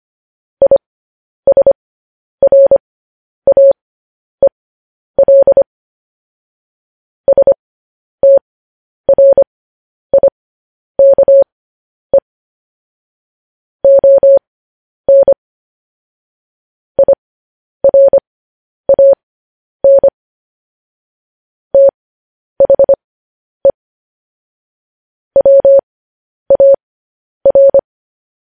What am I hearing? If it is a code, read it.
Morse code, heard as a keyed tone, about 25 wpm, spaced as for 10 wpm, 550 Hz. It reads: ISRAEL STRIKE ON IRAN THE WAR